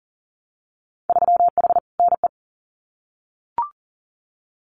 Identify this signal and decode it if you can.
Morse code, heard as a keyed tone, about 42 wpm, 700 Hz. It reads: 3H NE
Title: Winter Field Day Exchanges Encore 40wpm